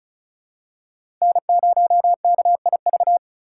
Morse code, heard as a keyed tone, about 34 wpm, 700 Hz. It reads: N0KIV
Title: USA Call Signs - 5 Characters - 4x Character Spacing - 35wpm